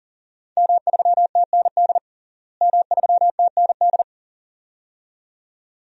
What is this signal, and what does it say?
Morse code, heard as a keyed tone, about 37 wpm, 700 Hz. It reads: M3TND M3TND
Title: DXCC-100 Call Signs - 5 Characters - Repeated 2x - 40wpm